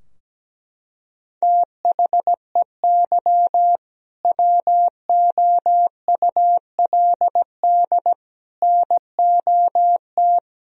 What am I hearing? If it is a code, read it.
Morse code, heard as a keyed tone, about 17 wpm, 700 Hz. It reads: THEY WOULD NOT
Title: Sentences from Top 1000 Words - Review Part 3 - 17wpm